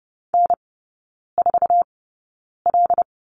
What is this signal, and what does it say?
Morse code, heard as a keyed tone, about 30 wpm, spaced as for 9 wpm, 700 Hz. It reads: N4L
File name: Lesson 17e - Letter u - Three Characters 30wpm